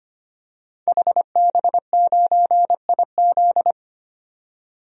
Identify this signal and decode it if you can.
Morse code, heard as a keyed tone, about 25 wpm, 700 Hz. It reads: HB9IZ
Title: DXCC-100 Call Signs - 5 Characters - 25wpm